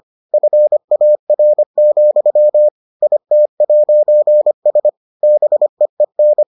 Morse code, heard as a keyed tone, about 25 wpm, 600 Hz. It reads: FAR, IT'S BEEN